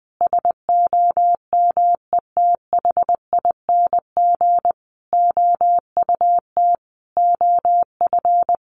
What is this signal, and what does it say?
Morse code, heard as a keyed tone, about 19 wpm, 700 Hz. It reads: SOMETHING OUT OF
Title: Sentences from Top 600 Words 20wpm